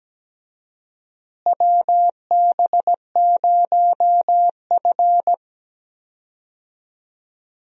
Morse code, heard as a keyed tone, about 17 wpm, 700 Hz. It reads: WB0F
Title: USA Call Signs - 4 Characters - Repeated 2x - 17wpm